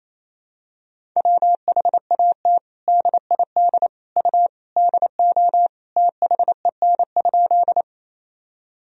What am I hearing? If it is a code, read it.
Morse code, heard as a keyed tone, about 28 wpm, 700 Hz. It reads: WHAT DID U DO THEN?
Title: Sentences from Top 100 Words and Abbreviations 28wpm